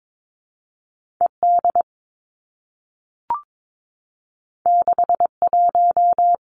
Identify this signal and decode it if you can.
Morse code, heard as a keyed tone, about 22 wpm, 700 Hz. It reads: E D 6 1
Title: Two Challenging Letter-Number Combinations 22wpm